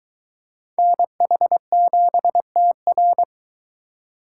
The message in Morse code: NH7TR

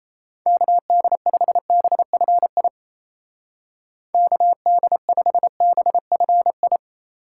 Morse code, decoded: KD5BFI KD5BFI

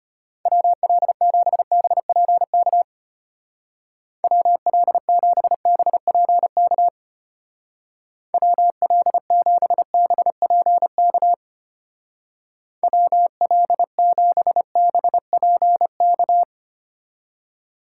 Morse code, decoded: WL7BPK WL7BPK WL7BPK WL7BPK

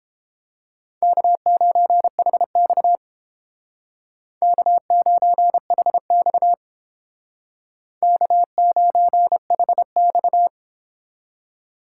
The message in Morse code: K9HX K9HX K9HX

